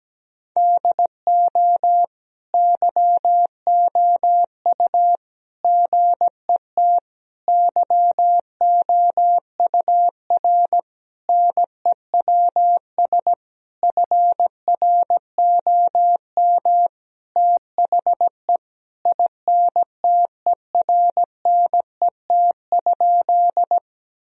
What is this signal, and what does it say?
Morse code, heard as a keyed tone, about 17 wpm, 700 Hz. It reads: DO YOU GET YOUR NEWS FROM THE INTERNET?
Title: Sentences from Top 1000 Words - Review Part 4 - 17wpm